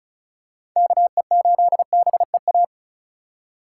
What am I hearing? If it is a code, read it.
Morse code, heard as a keyed tone, about 36 wpm, 700 Hz. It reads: KE8BEA
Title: USA Call Signs - 6 Characters - 3x Character Spacing - 35wpm